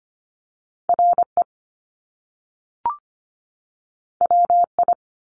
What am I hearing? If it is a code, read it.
Morse code, heard as a keyed tone, about 25 wpm, 700 Hz. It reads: RE WI
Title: Top Two Letter Combinations in Words 25wpm